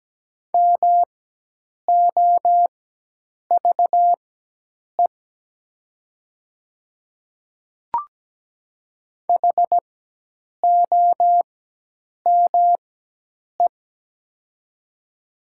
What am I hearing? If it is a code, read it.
Morse code, heard as a keyed tone, about 17 wpm, spaced as for 8 wpm, 700 Hz. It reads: MOVE HOME
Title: 4x Character Spacing - Top 200 Words - 17wpm